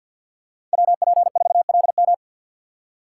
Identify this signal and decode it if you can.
Morse code, heard as a keyed tone, about 48 wpm, 700 Hz. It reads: WP4LM
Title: USA Call Signs - 4x Character Spacing - 50wpm